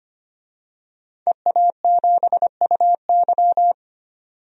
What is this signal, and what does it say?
Morse code, heard as a keyed tone, about 25 wpm, 700 Hz. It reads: EA7UY
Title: DXCC-100 Call Signs - 5 Characters - 25wpm